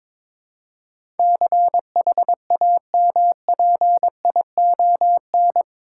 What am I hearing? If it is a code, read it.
Morse code, heard as a keyed tone, about 22 wpm, 700 Hz. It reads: CHAMPION
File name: practice / 22wpm / Eight Letter Words - 22wpm